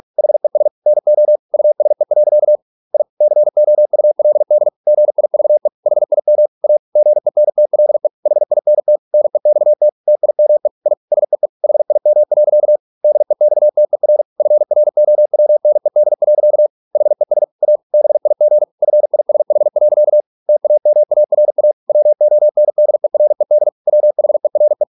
Morse code, heard as a keyed tone, about 46 wpm, 600 Hz. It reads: HES NO USE. I COULD GIVE HIM A GENTLE HINT NEXT TIME I SEE HIM. DEXTER FROWNED. HES A BIG FISH. TAMARA WONDERED WHERE